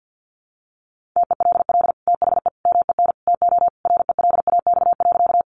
Morse code, heard as a keyed tone, about 49 wpm, 700 Hz. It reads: TELL THE MEN TO RELAX.
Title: Sentences from Top 1400 Words 50wpm